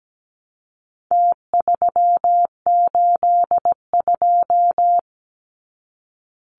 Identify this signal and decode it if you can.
Morse code, heard as a keyed tone, about 17 wpm, 700 Hz. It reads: T382
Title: Numbers - Contest Style - 4-Digit 17wpm